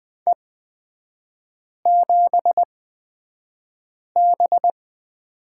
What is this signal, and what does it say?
Morse code, heard as a keyed tone, about 20 wpm, spaced as for 5 wpm, 700 Hz. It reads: E7B
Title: 250 Lesson 40e - BK - Three Characters 20wpm